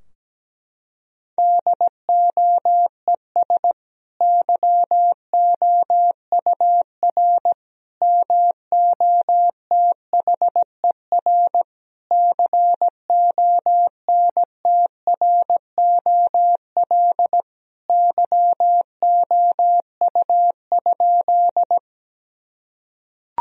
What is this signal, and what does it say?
Morse code, heard as a keyed tone, about 17 wpm, 700 Hz. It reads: DOES YOUR MOTHER CONTROL YOU?
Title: Sentences from Top 1000 Words - Review Part 3 - 17wpm